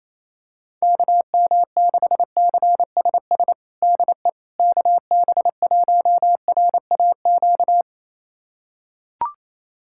Morse code, heard as a keyed tone, about 28 wpm, 700 Hz. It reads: KM6CSS DE KB1RAQ
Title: USA Callsign DE Callsign 28wpm